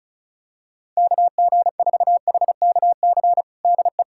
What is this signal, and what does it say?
Morse code, heard as a keyed tone, about 34 wpm, 700 Hz. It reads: KG4HKC DE
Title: USA Callsign DE Callsign 35wpm